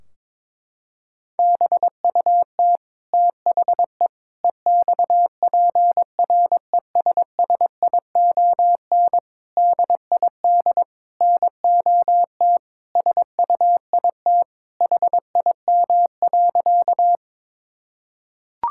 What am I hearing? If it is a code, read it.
Morse code, heard as a keyed tone, about 22 wpm, 700 Hz. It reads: BUT THE EXPRESSION DID NOT SUIT HIM.
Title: Sentences from Top 1600 Words 22wpm